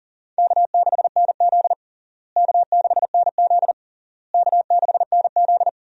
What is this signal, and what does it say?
Morse code, heard as a keyed tone, about 37 wpm, 700 Hz. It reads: K6NZ K6NZ K6NZ
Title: USA Call Signs - 4 Characters - Repeated 3x - 40wpm